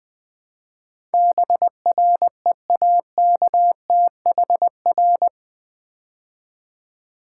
Morse code, heard as a keyed tone, about 21 wpm, 700 Hz. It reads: BREAKTHR
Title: Build A Word - 4 Letters At A Time - 12-Letter Words - 20wpm